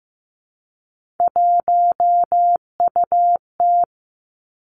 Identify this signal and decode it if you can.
Morse code, heard as a keyed tone, about 15 wpm, 700 Hz. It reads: 1UT
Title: Three Challenging Letter-Number Combinations 15wpm